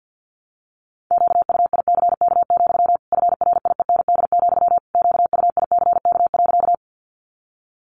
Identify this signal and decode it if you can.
Morse code, heard as a keyed tone, about 50 wpm, 700 Hz. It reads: QUICK, FRIEND, QUICK.
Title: Sentences from Top 500 Words 50wpm